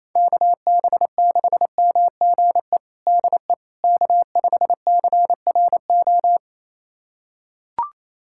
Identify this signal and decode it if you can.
Morse code, heard as a keyed tone, about 28 wpm, 700 Hz. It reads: KB6MGE DE K5CRO
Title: USA Callsign DE Callsign 28wpm